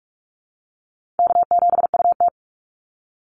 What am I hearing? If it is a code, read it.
Morse code, heard as a keyed tone, about 44 wpm, 700 Hz. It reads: K7UT
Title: Easy Call Signs - 45wpm